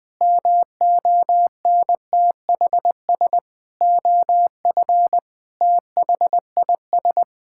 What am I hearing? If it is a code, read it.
Morse code, heard as a keyed tone, about 20 wpm, 700 Hz. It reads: MONTHS OF THIS
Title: Sets of 8 Words - 20wpm